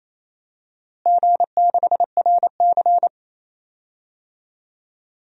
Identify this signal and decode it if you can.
Morse code, heard as a keyed tone, about 28 wpm, 700 Hz. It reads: G6RC
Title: DXCC-100 Call Signs - 4 Characters - Repeated 2x - 28wpm